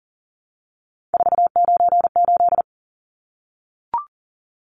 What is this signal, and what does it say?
Morse code, heard as a keyed tone, about 40 wpm, 700 Hz. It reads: 498
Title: Numbers - 3-Digit 40wpm